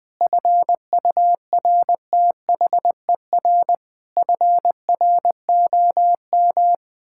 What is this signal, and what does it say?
Morse code, heard as a keyed tone, about 20 wpm, 700 Hz. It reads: FURTHER FROM
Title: Sentences with 8-Word Phrases - 20wpm